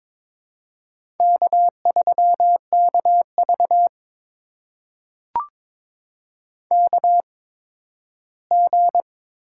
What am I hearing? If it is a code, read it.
Morse code, heard as a keyed tone, about 22 wpm, spaced as for 13 wpm, 700 Hz. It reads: K3KV KG
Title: USA Call Signs - 8x Character Spacing - 22wpm